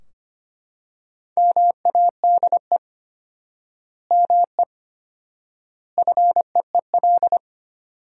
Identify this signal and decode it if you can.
Morse code, heard as a keyed tone, about 25 wpm, 700 Hz. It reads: MADE ME FEEL
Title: 4x Word-Spacing - Sentences from Top 400 Words - 25wpm